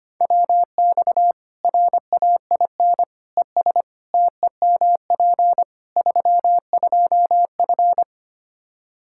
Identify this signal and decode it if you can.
Morse code, heard as a keyed tone, about 25 wpm, 700 Hz. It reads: WX RAIN ES TEMP 32F
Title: K1USN SST QSO Practice 25wpm